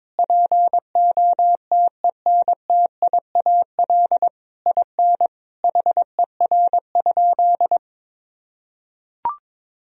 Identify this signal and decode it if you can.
Morse code, heard as a keyed tone, about 22 wpm, 700 Hz. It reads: POTENTIAL IN HER?
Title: Sentences from Top 1000 Words - Review Part 4 - 22wpm